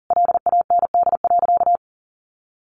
Morse code, heard as a keyed tone, about 40 wpm, 700 Hz. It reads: LAND.
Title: Sentences from Top 1100 Words 40wpm